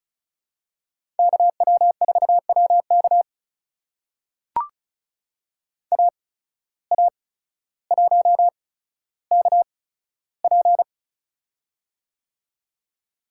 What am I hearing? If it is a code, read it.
Morse code, heard as a keyed tone, about 36 wpm, spaced as for 15 wpm, 700 Hz. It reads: KW4WK AA1KP
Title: USA Call Signs - 5 Characters - 8x Character Spacing - 35wpm